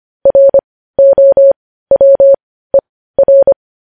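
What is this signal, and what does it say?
Morse code, heard as a keyed tone, about 25 wpm, spaced as for 15 wpm, 550 Hz. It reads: ROWER
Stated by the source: Morse code news headlines at 15 WPM.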